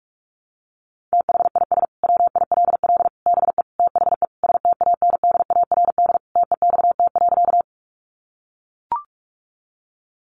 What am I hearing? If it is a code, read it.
Morse code, heard as a keyed tone, about 44 wpm, 700 Hz. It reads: THIS WILL BE THE STANDARD TEXT.
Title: Sentences from Top 1200 Words 45wpm